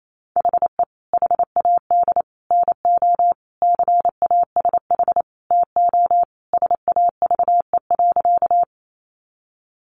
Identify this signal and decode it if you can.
Morse code, heard as a keyed tone, about 28 wpm, 700 Hz. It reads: HE HAD NO CASH TO SAVE.
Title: Sentences from Top 1100 Words 28wpm